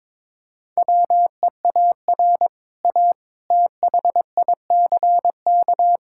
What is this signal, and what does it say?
Morse code, heard as a keyed tone, about 21 wpm, 700 Hz. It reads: WEAR A THICK
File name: practice / 22wpm / Easy Sentences with Hints - 22wpm